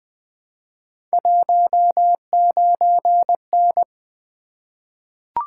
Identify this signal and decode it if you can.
Morse code, heard as a keyed tone, about 20 wpm, 700 Hz. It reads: 19N
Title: Three Letter-Number Combinations - 20wpm